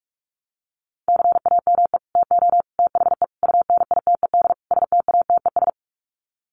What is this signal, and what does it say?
Morse code, heard as a keyed tone, about 44 wpm, 700 Hz. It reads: CAME TO THE UNITED STATES
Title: Sets of 5 Words Encore2 45wpm